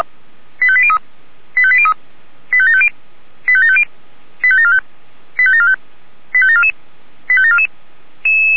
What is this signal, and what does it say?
Five-tone selective call (ZVEI), heard as quick runs of five tones, tones of about 70 ms. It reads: 85803 85803 85750 85750 85745 85745 85744 85744